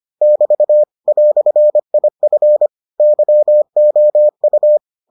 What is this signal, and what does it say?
Morse code, heard as a keyed tone, about 25 wpm, 600 Hz. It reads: <BT> "IF YOU